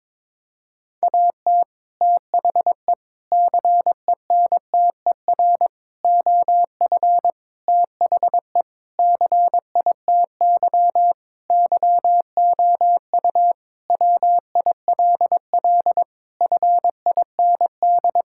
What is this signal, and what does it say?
Morse code, heard as a keyed tone, about 22 wpm, 700 Hz. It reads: AT THE CENTER OF THE CITY YOU WILL FIND
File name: practice / 22wpm / Sentences with 4-Word Phrases - 22wpm